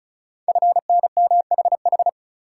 Morse code, heard as a keyed tone, about 36 wpm, 690 Hz. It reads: FNMHH
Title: Code Groups - 5 Letters - 35wpm